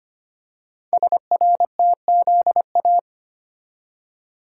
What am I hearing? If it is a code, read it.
Morse code, heard as a keyed tone, about 25 wpm, 690 Hz. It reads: SRTZA
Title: Code Groups - 5 Letters - 25wpm